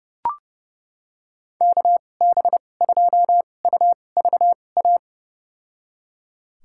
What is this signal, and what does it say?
Morse code, heard as a keyed tone, about 32 wpm, spaced as for 22 wpm, 700 Hz. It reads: KB2UVA
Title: USA Call Signs - 6 Characters - 2x Character Spacing - 30wpm